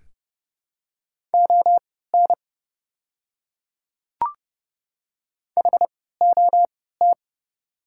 Morse code, heard as a keyed tone, about 30 wpm, spaced as for 17 wpm, 700 Hz. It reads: ON HOT
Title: 3x Character Spacing - Top 100 Words - 30wpm